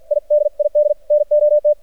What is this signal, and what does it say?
Morse code, heard as a keyed tone, about 50 wpm, 600 Hz, with radio noise. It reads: I DID NOT
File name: archived / CRSnet